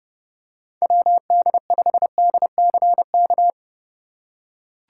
Morse code, heard as a keyed tone, about 31 wpm, 700 Hz. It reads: WD5DCK
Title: USA Call Signs - 3x Character Spacing - 30wpm